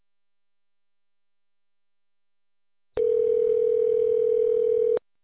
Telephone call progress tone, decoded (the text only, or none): ringback tone